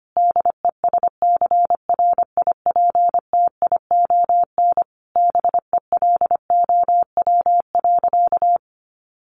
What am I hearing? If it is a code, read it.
Morse code, heard as a keyed tone, about 25 wpm, 700 Hz. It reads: DESCRIPTION BELOW.
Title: Sentences from Top 1500 Words 25wpm